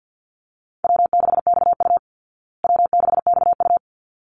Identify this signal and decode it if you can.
Morse code, heard as a keyed tone, about 50 wpm, 700 Hz. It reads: W6XU W6XU
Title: Call Signs Repeated 3x - 50wpm